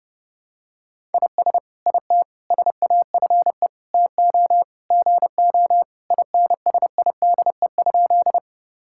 Morse code, heard as a keyed tone, about 32 wpm, 700 Hz. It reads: IS IT SAFE TO GO INSIDE?
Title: Sentences from Top 1000 Words - Review Part 6 - 30wpm